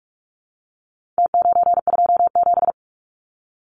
Morse code, heard as a keyed tone, about 44 wpm, 700 Hz. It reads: T927
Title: Numbers - Contest Style - 4-Digit 45wpm